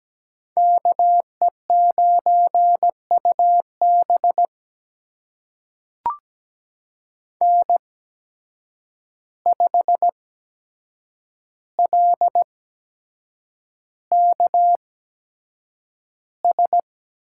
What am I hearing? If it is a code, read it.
Morse code, heard as a keyed tone, about 17 wpm, spaced as for 7 wpm, 700 Hz. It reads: KE9UB N5LKS
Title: USA Call Signs - 5 Characters - 8x Character Spacing - 17wpm